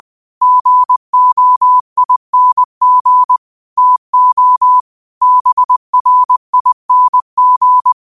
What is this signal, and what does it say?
Morse code, heard as a keyed tone, about 20 wpm, 1000 Hz. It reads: GOING TO BRING